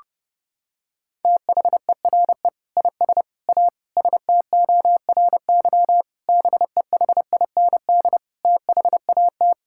Morse code, heard as a keyed tone, about 32 wpm, 700 Hz. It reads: THERE IS A STORY BEHIND THAT
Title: Sentences from Top 1000 Words - Review Part 5 - 30wpm